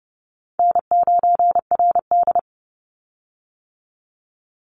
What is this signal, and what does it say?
Morse code, heard as a keyed tone, about 30 wpm, 700 Hz. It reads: N9RD
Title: Easy Call Signs - 30wpm